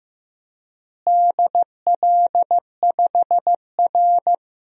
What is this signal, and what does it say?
Morse code, heard as a keyed tone, about 15 wpm, 700 Hz. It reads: DL5R
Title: DXCC-100 Call Signs - 4 Characters - Repeated 3x - 15wpm